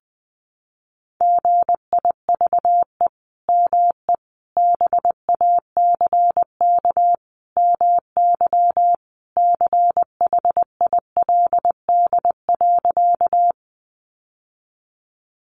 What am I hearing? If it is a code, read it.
Morse code, heard as a keyed tone, about 19 wpm, 700 Hz. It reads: GIVE ME BACK MY CHILD.
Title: Sentences from Top 800 Words 20wpm